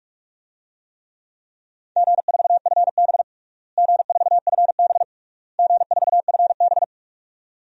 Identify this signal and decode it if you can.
Morse code, heard as a keyed tone, about 46 wpm, 700 Hz. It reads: G4FB G4FB G4FB
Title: DXCC-100 Call Signs - 4 Characters - Repeated 3x - 45wpm